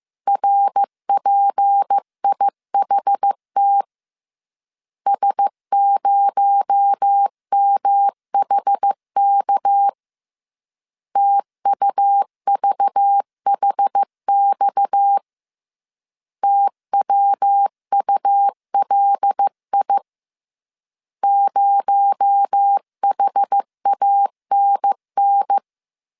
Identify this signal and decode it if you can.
Morse code, heard as a keyed tone, about 15 wpm, 800 Hz. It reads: RPIHT S0MHK TUVHX TWULI 0HANN